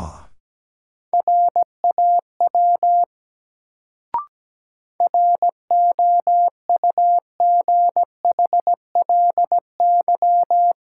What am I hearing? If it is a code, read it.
Morse code, heard as a keyed tone, about 17 wpm, 700 Hz. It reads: RAW ROUGHLY
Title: Sentences from Top 1500 Words - New Word Drill Alt - 17wpm